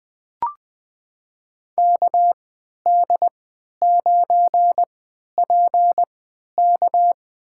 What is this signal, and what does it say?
Morse code, heard as a keyed tone, about 20 wpm, spaced as for 11 wpm, 700 Hz. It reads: KD9PK
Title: USA Call Signs - 5 Characters - 3x Character Spacing - 20wpm